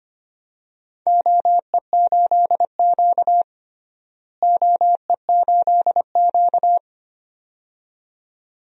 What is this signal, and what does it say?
Morse code, heard as a keyed tone, about 25 wpm, 700 Hz. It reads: OE8Q OE8Q
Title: DXCC-100 Call Signs - 4 Characters - Repeated 2x - 25wpm